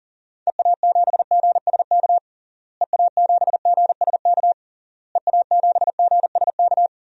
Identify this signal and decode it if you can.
Morse code, heard as a keyed tone, about 40 wpm, 700 Hz. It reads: EA7GSK EA7GSK EA7GSK
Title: DXCC-100 Call Signs - 6 Characters - Repeated 3x - 40wpm